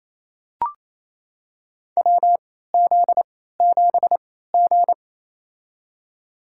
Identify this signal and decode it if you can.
Morse code, heard as a keyed tone, about 28 wpm, spaced as for 16 wpm, 700 Hz. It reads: WZ7G
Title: USA Call Signs - 3x Character Spacing - 28wpm